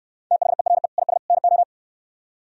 Morse code, heard as a keyed tone, about 66 wpm, 700 Hz. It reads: THERE IS NO